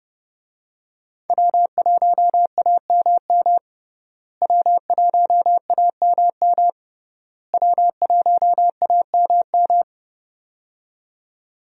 Morse code, heard as a keyed tone, about 30 wpm, 700 Hz. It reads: W1AMM W1AMM W1AMM
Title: USA Call Signs - 5 Characters - Repeated 3x - 30wpm